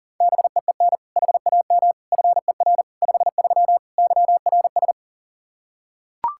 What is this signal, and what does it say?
Morse code, heard as a keyed tone, about 40 wpm, 690 Hz. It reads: BEEN HAM FER 53 YRS
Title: Practice QSO - Exchange 3 - More Info - 40wpm